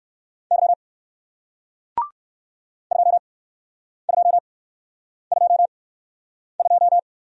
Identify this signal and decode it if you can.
Morse code, heard as a keyed tone, about 58 wpm, spaced as for 9 wpm, 700 Hz. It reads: Y 2222